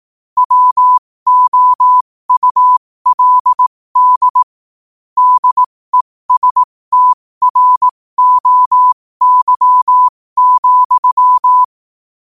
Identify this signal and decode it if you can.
Morse code, heard as a keyed tone, about 18 wpm, spaced as for 15 wpm, 1000 Hz. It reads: WOULD DESTROY,
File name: qotdfiles / 15wpm